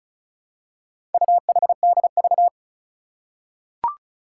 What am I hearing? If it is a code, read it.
Morse code, heard as a keyed tone, about 36 wpm, 700 Hz. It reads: UHDV
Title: Four Challenging Letter-Number Combinations 35wpm